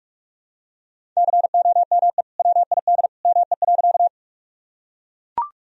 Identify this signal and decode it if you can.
Morse code, heard as a keyed tone, about 42 wpm, 700 Hz. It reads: COME WID ME.
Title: Sentences from Top 200 Words and Abbreviations 45wpm